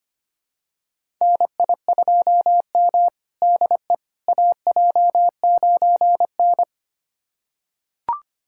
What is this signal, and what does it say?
Morse code, heard as a keyed tone, about 25 wpm, 700 Hz. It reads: NI2M DE AJ9N
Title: USA Callsign DE Callsign - 4 Characters - 25wpm